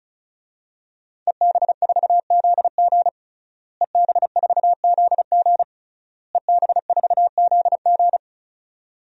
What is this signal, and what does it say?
Morse code, heard as a keyed tone, about 36 wpm, 700 Hz. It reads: EB4ZG EB4ZG EB4ZG
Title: DXCC-100 Call Signs - 5 Characters - Repeated 3x - 35wpm